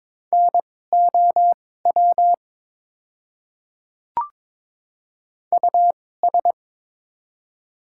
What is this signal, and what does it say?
Morse code, heard as a keyed tone, about 21 wpm, spaced as for 16 wpm, 700 Hz. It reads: NOW US
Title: 2x Character Spacing - Top 200 Words - 22wpm